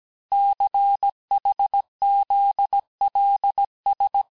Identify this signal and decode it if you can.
Morse code, heard as a keyed tone, about 17 wpm, 800 Hz. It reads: CHZLS
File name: cours-du-lundi files / cours-1587-texte-chiffre-vitesse-16MPM